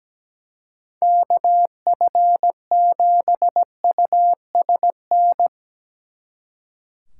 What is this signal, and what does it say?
Morse code, heard as a keyed tone, about 17 wpm, 700 Hz. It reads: KF7USN
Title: Speed Racing - USA 2x3 Call Signs 17wpm